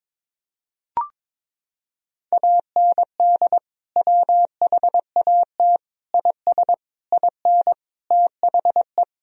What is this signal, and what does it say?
Morse code, heard as a keyed tone, about 22 wpm, 700 Hz. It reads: AND WHAT IS IN THE
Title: Sentences from Top 1000 Words - Review Part 1 - 22wpm